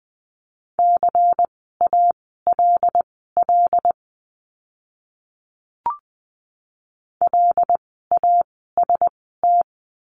Morse code, heard as a keyed tone, about 20 wpm, spaced as for 15 wpm, 700 Hz. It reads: CALL LAST
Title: 094 Lesson 16f - Letter c - Words 20wpm